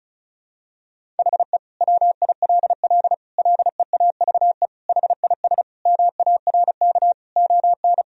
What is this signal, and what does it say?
Morse code, heard as a keyed tone, about 34 wpm, 700 Hz. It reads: HE WILL LEAVE HIS MARK ON
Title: Sentences from Top 1000 Words - Review Part 1 - 35wpm